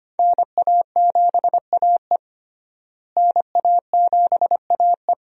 Morse code, heard as a keyed tone, about 25 wpm, 700 Hz. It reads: NA7AE NA7AE
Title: USA Call Signs - 5 Characters - Repeated 2x - 25wpm